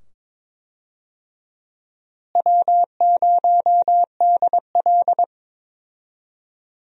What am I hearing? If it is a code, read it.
Morse code, heard as a keyed tone, about 21 wpm, 700 Hz. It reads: W0DL
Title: USA Call Signs - 4 Characters - Repeated 2x - 22wpm